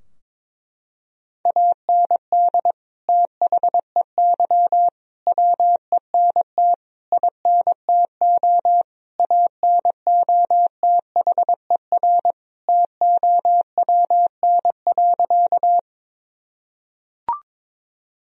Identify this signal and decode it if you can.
Morse code, heard as a keyed tone, about 22 wpm, 700 Hz. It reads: AND THEY WENT INTO ANOTHER TOWN.
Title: Sentences from Top 1000 Words - Review Part 2 - 22wpm